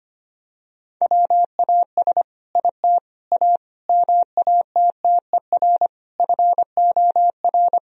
Morse code, heard as a keyed tone, about 25 wpm, 700 Hz. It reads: WAS IT A MATTER FOR
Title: Sentences from Top 1000 Words - Review Part 4 - 25wpm